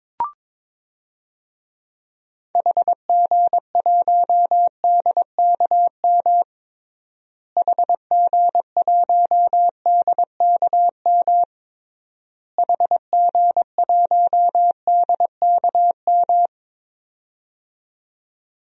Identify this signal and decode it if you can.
Morse code, heard as a keyed tone, about 22 wpm, 700 Hz. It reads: HG1DKM HG1DKM HG1DKM
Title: DXCC-100 Call Signs - 6 Characters - Repeated 3x - 22wpm